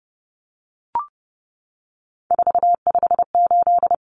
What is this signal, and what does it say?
Morse code, heard as a keyed tone, about 29 wpm, 700 Hz. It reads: 458
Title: Numbers - 3-Digit 30wpm